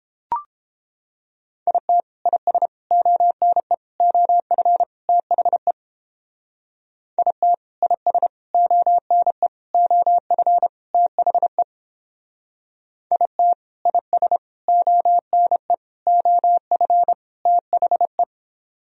Morse code, heard as a keyed tone, about 31 wpm, 700 Hz. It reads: IT IS ONE OF THE IT IS ONE OF THE IT IS ONE OF THE